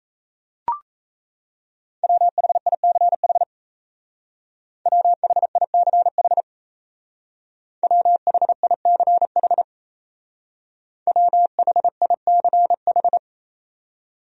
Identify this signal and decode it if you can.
Morse code, heard as a keyed tone, about 36 wpm, 690 Hz. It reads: WHICH WHICH WHICH WHICH